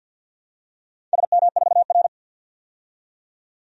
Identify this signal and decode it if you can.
Morse code, heard as a keyed tone, about 52 wpm, 700 Hz. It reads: SM4R
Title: DXCC-100 Call Signs - 4 Characters - 50wpm